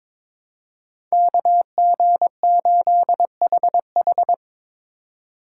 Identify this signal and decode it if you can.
Morse code, heard as a keyed tone, about 23 wpm, 700 Hz. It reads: KG8HH
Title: USA Call Signs - 5 Characters - 4x Character Spacing - 22wpm